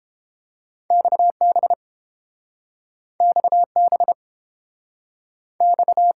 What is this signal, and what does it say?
Morse code, heard as a keyed tone, about 30 wpm, 700 Hz. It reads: XB XB X